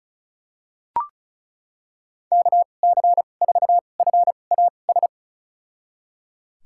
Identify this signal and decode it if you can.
Morse code, heard as a keyed tone, about 36 wpm, spaced as for 25 wpm, 700 Hz. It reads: KC4FAS
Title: USA Call Signs - 6 Characters - 2x Character Spacing - 35wpm